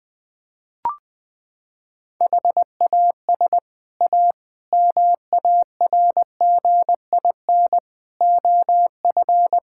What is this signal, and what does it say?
Morse code, heard as a keyed tone, about 21 wpm, 700 Hz. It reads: HAS A MARGIN OF
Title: Sets of 6 Words - 20wpm